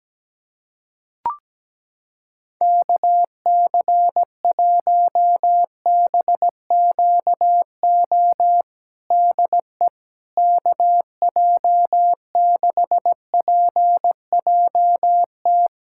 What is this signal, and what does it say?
Morse code, heard as a keyed tone, about 17 wpm, 700 Hz. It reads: KC1BQO DE KJ6PJT
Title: USA Callsign DE Callsign - 6 Characters - 17wpm